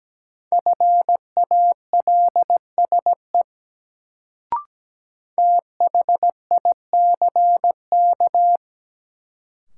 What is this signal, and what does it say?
Morse code, heard as a keyed tone, about 17 wpm, 700 Hz. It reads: FALSE THICK
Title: Sentences from Top 1600 Words - New Word Drill - 17wpm